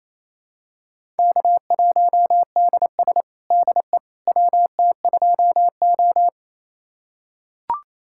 Morse code, heard as a keyed tone, about 28 wpm, 700 Hz. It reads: K1DS DE WT2O
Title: USA Callsign DE Callsign - 4 Characters - 28wpm